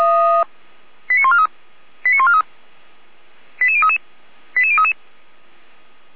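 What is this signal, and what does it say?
Five-tone selective call (ZVEI), heard as quick runs of five tones, tones of about 70 ms. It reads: 89143 89143 88033 88033
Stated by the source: Five-tone ZVEI selective call received on an SDR.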